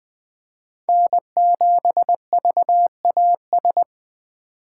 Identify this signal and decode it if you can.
Morse code, heard as a keyed tone, about 20 wpm, 700 Hz. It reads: N7VAS
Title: USA Call Signs - 2x Character Spacing - 20wpm